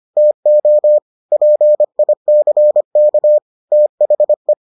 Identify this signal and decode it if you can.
Morse code, heard as a keyed tone, about 25 wpm, 600 Hz. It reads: TO PICK THE